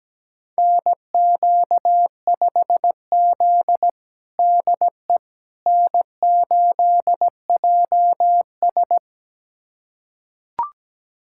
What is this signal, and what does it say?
Morse code, heard as a keyed tone, about 17 wpm, 700 Hz. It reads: NQ5Z DE N8JS
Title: USA Callsign DE Callsign - 4 Characters - 17wpm